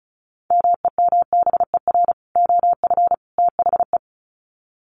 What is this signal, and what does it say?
Morse code, heard as a keyed tone, about 34 wpm, 700 Hz. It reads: MEMBER OF THE
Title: Sets of 3 Words 35wpm